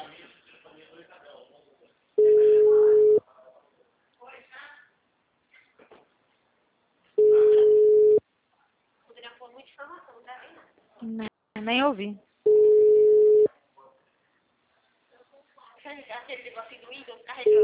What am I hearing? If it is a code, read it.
Ringback tone (Europe, 425 Hz), with speech.